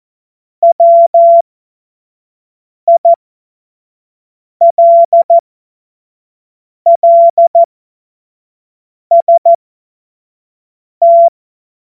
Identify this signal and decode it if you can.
Morse code, heard as a keyed tone, about 14 wpm, spaced as for 5 wpm, 680 Hz. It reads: WILLST
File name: Morsekurs / Witze / Witz 22. Februar 2022 25 F70